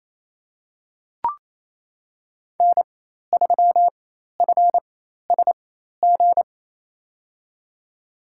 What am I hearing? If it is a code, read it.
Morse code, heard as a keyed tone, about 28 wpm, spaced as for 13 wpm, 700 Hz. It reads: N3FSG
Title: USA Call Signs - 4x Character Spacing - 28wpm